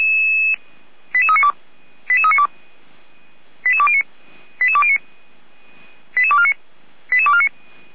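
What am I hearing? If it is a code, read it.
Five-tone selective call (ZVEI), heard as quick runs of five tones, tones of about 70 ms. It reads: 88392 88392 88209 88209 88249 88249
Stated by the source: Received on SDR; ZVEI five-tone selective call.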